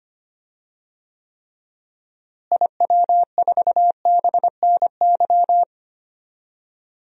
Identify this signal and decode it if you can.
Morse code, heard as a keyed tone, about 25 wpm, 700 Hz. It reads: IW4BNY
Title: DXCC-100 Call Signs - 6 Characters - Repeated 3x - 25wpm